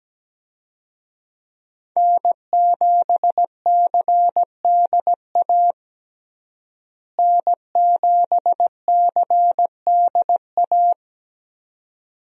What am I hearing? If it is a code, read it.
Morse code, heard as a keyed tone, about 17 wpm, 700 Hz. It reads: N7CDA N7CDA
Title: USA Call Signs - 5 Characters - Repeated 2x - 17wpm